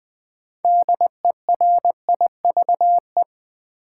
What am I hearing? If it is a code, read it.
Morse code, heard as a keyed tone, about 20 wpm, 700 Hz. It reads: DERIVE
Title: Build A Word - 2 Letters At A Time - 6-Letter Words - 20wpm